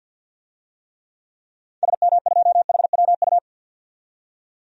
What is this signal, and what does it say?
Morse code, heard as a keyed tone, about 46 wpm, 700 Hz. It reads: SM2HWU
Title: DXCC-100 Call Signs - 6 Characters - 50wpm